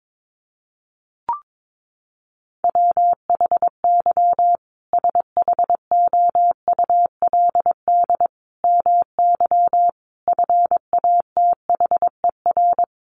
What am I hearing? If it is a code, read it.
Morse code, heard as a keyed tone, about 22 wpm, 700 Hz. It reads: WHY SHOULD MY FATHER
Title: Sentences from Top 500 Words 22wpm